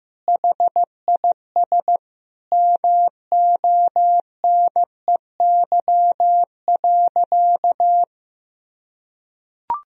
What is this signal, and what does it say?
Morse code, heard as a keyed tone, about 15 wpm, 700 Hz. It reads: HIS MONEY.
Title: Sentences from Top 1000 Words - Review Part 4 - 15wpm